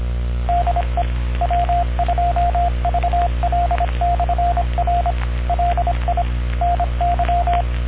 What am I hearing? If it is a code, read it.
Morse code, heard as a keyed tone, about 25 wpm, 700 Hz, with noise. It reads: DE W2VL/R LI NY